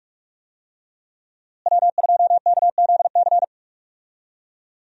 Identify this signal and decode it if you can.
Morse code, heard as a keyed tone, about 46 wpm, 700 Hz. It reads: W2KZC
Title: USA Call Signs - 5 Characters - Repeated 3x - 45wpm